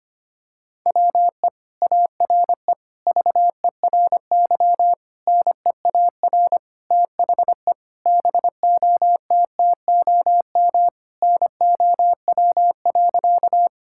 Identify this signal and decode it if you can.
Morse code, heard as a keyed tone, about 25 wpm, 700 Hz. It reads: WE ARE VERY NEAR THE BOTTOM NOW.
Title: Sentences from Top 1000 Words - Review Part 6 - 25wpm